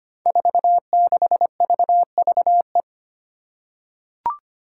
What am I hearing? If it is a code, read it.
Morse code, heard as a keyed tone, about 25 wpm, 700 Hz. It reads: 46VVE
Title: Five Challenging Letter-Number Combinations 25wpm